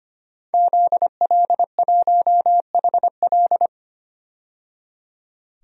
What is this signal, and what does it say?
Morse code, heard as a keyed tone, about 25 wpm, 700 Hz. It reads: ZL1HL
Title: DXCC-100 Call Signs - 5 Characters - 25wpm